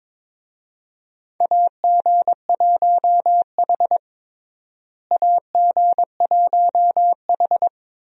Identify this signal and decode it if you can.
Morse code, heard as a keyed tone, about 22 wpm, 700 Hz. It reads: AG1H AG1H
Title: USA Call Signs - 4 Characters - Repeated 2x - 22wpm